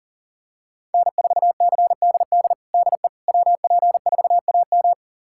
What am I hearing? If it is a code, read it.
Morse code, heard as a keyed tone, about 38 wpm, 700 Hz. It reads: N4CDD DE WP4AM
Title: USA Callsign DE Callsign - 5 Characters - 40wpm